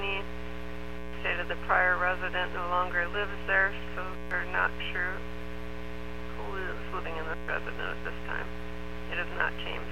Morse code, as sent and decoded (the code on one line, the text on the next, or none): none